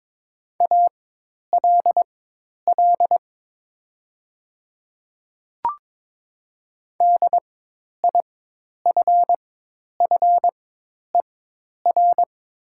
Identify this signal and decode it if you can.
Morse code, heard as a keyed tone, about 22 wpm, spaced as for 10 wpm, 700 Hz. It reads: ALL DIFFER
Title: 4x Character Spacing - Top 500 Words - 22wpm